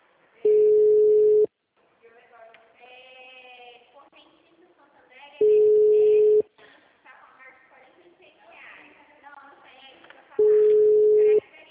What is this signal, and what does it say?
Ringback tone (Europe, 425 Hz), with speech.
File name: out- -2013- - - .58247